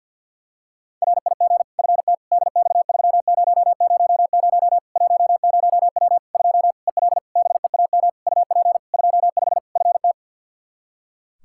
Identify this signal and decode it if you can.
Morse code, heard as a keyed tone, about 48 wpm, 700 Hz. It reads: RIG FT DX3000 10W 2 EL BEAM UP 25 FT
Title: Practice QSO - Exchange 2 - Info - 50wpm